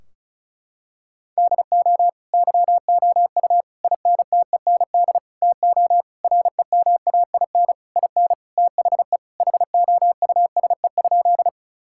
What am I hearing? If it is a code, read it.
Morse code, heard as a keyed tone, about 34 wpm, 700 Hz. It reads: DO YOU INTEND TO REMAIN IN THE HOUSE?
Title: Sentences from Top 1600 Words 35wpm